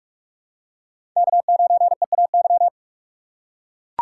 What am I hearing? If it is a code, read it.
Morse code, heard as a keyed tone, about 44 wpm, 700 Hz. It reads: K9EAY